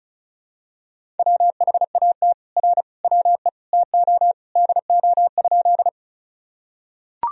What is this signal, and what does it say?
Morse code, heard as a keyed tone, about 36 wpm, 700 Hz. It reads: WHAT R WE TO DO?
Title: Sentences from Top 100 Words and Abbreviations 35wpm